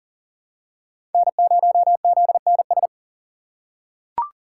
Morse code, heard as a keyed tone, about 38 wpm, 700 Hz. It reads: N0ZNS
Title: USA Call Signs - 5 Characters - 3x Character Spacing - 40wpm